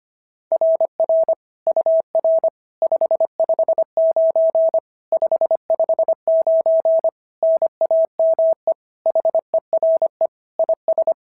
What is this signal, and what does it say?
Morse code, heard as a keyed tone, about 25 wpm, 690 Hz. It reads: RR UR 559 559 NAME HERE IS
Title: SOTA QSO Practice 25wpm